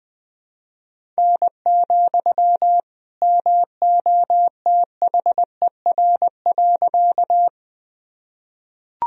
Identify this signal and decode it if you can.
Morse code, heard as a keyed tone, about 20 wpm, 700 Hz. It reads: N, MOTHER.
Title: Sentences from Top 200 Words and Abbreviations 20wpm